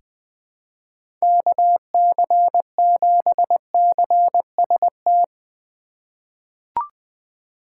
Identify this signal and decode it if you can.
Morse code, heard as a keyed tone, about 20 wpm, 700 Hz. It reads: KC7CST
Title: USA Call Signs - 6 Characters - 2x Character Spacing - 20wpm